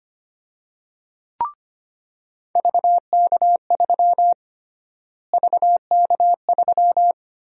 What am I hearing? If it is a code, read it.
Morse code, heard as a keyed tone, about 25 wpm, 700 Hz. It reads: VK3 VK3
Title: USA Call Sign Prefixes - Repeated 2x - 25wpm